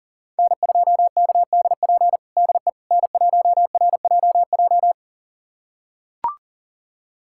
Morse code, heard as a keyed tone, about 40 wpm, 700 Hz. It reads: N2KDP DE N1RJJ
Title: USA Callsign DE Callsign - 5 Characters - 40wpm